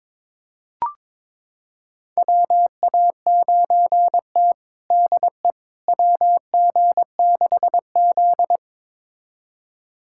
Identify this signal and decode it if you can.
Morse code, heard as a keyed tone, about 22 wpm, 700 Hz. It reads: WA9T DE WG6Z
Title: USA Callsign DE Callsign - 4 Characters - 22wpm